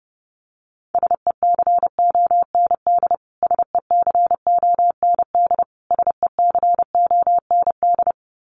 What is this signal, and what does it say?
Morse code, heard as a keyed tone, about 30 wpm, 700 Hz. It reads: SECOND SECOND SECOND
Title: Top 500 Words Repeated 30wpm